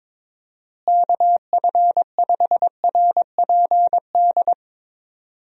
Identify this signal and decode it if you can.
Morse code, heard as a keyed tone, about 21 wpm, 700 Hz. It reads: KF5RPD
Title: USA Call Signs - 6 Characters - Repeated 3x - 22wpm